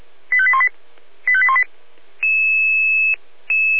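Five-tone selective call (ZVEI), heard as quick runs of five tones, tones of about 70 ms. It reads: 85818 85818
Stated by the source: Five-tone ZVEI selective call received on an SDR, from feuerwehr-orlamuende (fire brigade Orlamünde).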